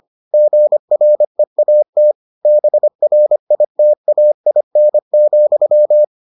Morse code, heard as a keyed tone, about 25 wpm, 600 Hz. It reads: GREAT BRITAIN,